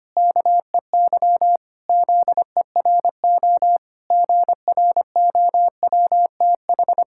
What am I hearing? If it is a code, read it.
Morse code, heard as a keyed tone, about 25 wpm, 700 Hz. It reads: KEY ZERO GROWTH
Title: Pangram Challenge v2 - 25wpm